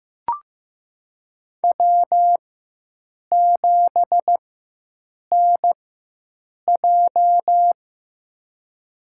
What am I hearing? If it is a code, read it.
Morse code, heard as a keyed tone, about 15 wpm, spaced as for 7 wpm, 700 Hz. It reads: W7NJ
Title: USA Call Signs - 4 Characters - 4x Character Spacing - 15wpm